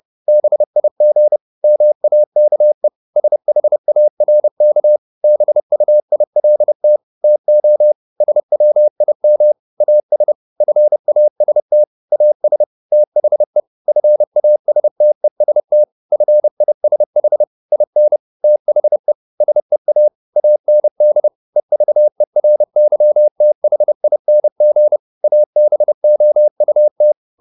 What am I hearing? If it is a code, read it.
Morse code, heard as a keyed tone, about 30 wpm, 600 Hz. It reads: BIG MAKE SHARK BUILT TO SWIM AS FAST AS THE FASTEST FISH IN THE SEA AND EVERYTHING ABOUT